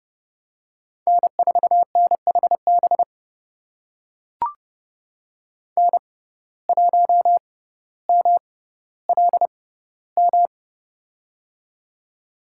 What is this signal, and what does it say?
Morse code, heard as a keyed tone, about 30 wpm, spaced as for 15 wpm, 700 Hz. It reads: N4NHB N1MLM
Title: USA Call Signs - 6x Character Spacing - 30wpm